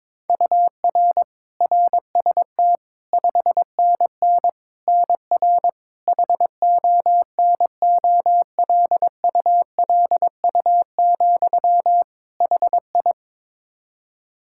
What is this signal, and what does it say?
Morse code, heard as a keyed tone, about 22 wpm, 700 Hz. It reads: UR RST 5NN NR HONOLULU, HI
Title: Practice QSO - Exchange 1 - Beginning - 22wpm